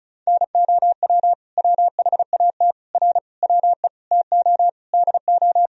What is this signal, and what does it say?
Morse code, heard as a keyed tone, about 34 wpm, 700 Hz. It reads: NOW WHAT R WE TO DO
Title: Sentences from Top 100 Words and Abbreviations 35wpm